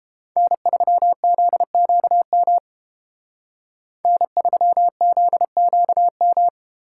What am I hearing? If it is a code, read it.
Morse code, heard as a keyed tone, about 30 wpm, 700 Hz. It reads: N3ZQM N3ZQM